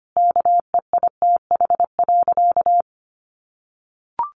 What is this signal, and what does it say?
Morse code, heard as a keyed tone, about 25 wpm, 700 Hz. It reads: KEITH.
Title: A Sound Of Thunder 25wpm